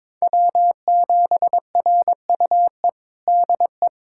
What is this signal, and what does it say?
Morse code, heard as a keyed tone, about 22 wpm, 700 Hz. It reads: W7RUE DE
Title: USA Callsign DE Callsign - 5 Characters - 22wpm